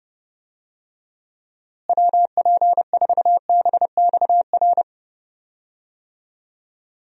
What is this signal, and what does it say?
Morse code, heard as a keyed tone, about 30 wpm, 700 Hz. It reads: WP4BXR
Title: USA Call Signs - 6 Characters - Repeated 2x - 30wpm